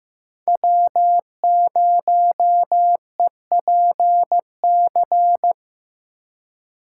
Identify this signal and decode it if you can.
Morse code, heard as a keyed tone, about 15 wpm, 700 Hz. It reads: W0EPC
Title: K1USN SST QSO Practice - 15wpm